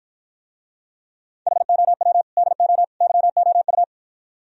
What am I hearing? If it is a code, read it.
Morse code, heard as a keyed tone, about 54 wpm, 700 Hz. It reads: HOW DO YOU